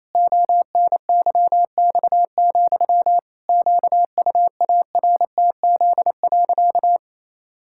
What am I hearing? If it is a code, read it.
Morse code, heard as a keyed tone, about 28 wpm, 700 Hz. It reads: ONYX, QUARTZ.
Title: Pangram Challenge v2 - 28wpm